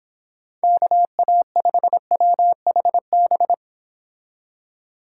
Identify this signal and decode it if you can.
Morse code, heard as a keyed tone, about 26 wpm, 700 Hz. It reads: KA5WHB